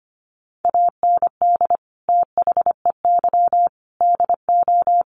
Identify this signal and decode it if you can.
Morse code, heard as a keyed tone, about 25 wpm, 700 Hz. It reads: AND THEY DO
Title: Sets of 4 Words 25wpm